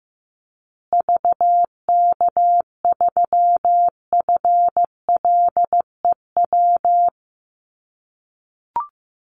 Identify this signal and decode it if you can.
Morse code, heard as a keyed tone, about 15 wpm, 700 Hz. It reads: VK3FLEW
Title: USA Call Signs Repeated 3x 15wpm